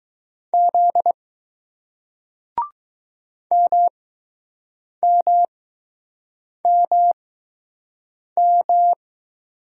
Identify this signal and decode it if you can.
Morse code, heard as a keyed tone, about 21 wpm, spaced as for 6 wpm, 700 Hz. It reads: Z MMMM